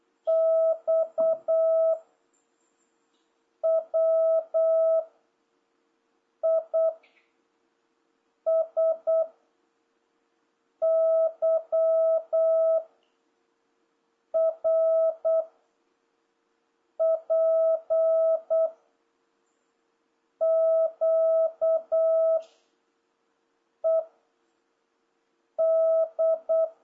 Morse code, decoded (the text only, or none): X W I S Y R P Q E D